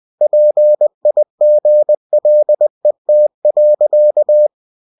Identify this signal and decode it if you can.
Morse code, heard as a keyed tone, about 20 wpm, 600 Hz. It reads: PIGLET.
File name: cw-books / winnie-the-pooh / winnie-20wpm-page-0054